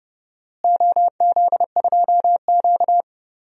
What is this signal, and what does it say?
Morse code, heard as a keyed tone, about 30 wpm, 700 Hz. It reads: OZ2Q
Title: DXCC-100 Call Signs - 4 Characters - Repeated 3x - 30wpm